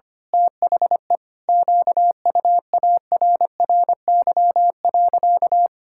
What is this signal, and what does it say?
Morse code, heard as a keyed tone, about 25 wpm, 700 Hz. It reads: THE QUARRY.
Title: Pangram Challenge v2 - 25wpm